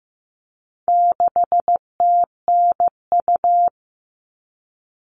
Morse code, heard as a keyed tone, about 15 wpm, 700 Hz. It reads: 6TNU
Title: All Cut Numbers - 4-Digit 15wpm